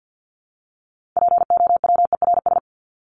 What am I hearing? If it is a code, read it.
Morse code, heard as a keyed tone, about 50 wpm, 700 Hz. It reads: POWERS
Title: ES:S Suffix - Top 100 Words with Base Words 50wpm